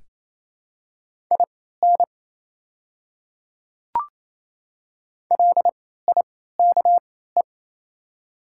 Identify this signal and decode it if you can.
Morse code, heard as a keyed tone, about 28 wpm, spaced as for 16 wpm, 700 Hz. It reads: IN LIKE
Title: 3x Character Spacing - Top 100 Words - 28wpm